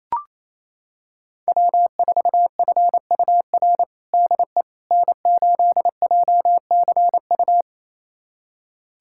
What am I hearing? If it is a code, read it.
Morse code, heard as a keyed tone, about 28 wpm, 700 Hz. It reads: W4FUR DE N8JCU
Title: USA Callsign DE Callsign - 5 Characters - 28wpm